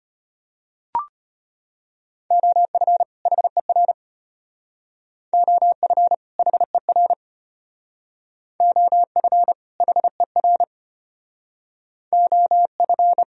Morse code, heard as a keyed tone, about 32 wpm, 700 Hz. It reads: OF HER OF HER OF HER OF